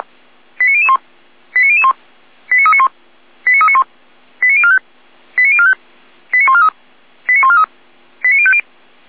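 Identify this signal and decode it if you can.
Five-tone selective call (ZVEI), heard as quick runs of five tones, tones of about 70 ms. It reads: 89001 89001 89391 89391 89045 89045 89143 89143 89050
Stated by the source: ZVEI five-tone selective call received on an SDR.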